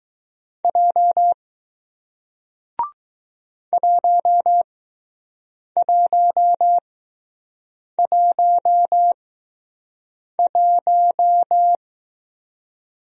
Morse code, heard as a keyed tone, about 20 wpm, spaced as for 6 wpm, 700 Hz. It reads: J 1111